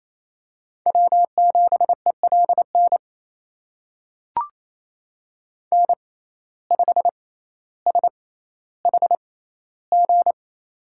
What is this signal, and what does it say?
Morse code, heard as a keyed tone, about 28 wpm, spaced as for 14 wpm, 700 Hz. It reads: W7ELN N5SHG
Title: USA Call Signs - 5 Characters - 6x Character Spacing - 28wpm